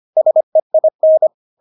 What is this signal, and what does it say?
Morse code, heard as a keyed tone, about 25 wpm, 630 Hz. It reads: SEIN
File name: Witz6 25wpm